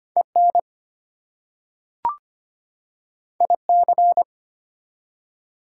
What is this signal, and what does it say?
Morse code, heard as a keyed tone, about 25 wpm, 700 Hz. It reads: EN IC
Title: Prefixes and Suffixes 25wpm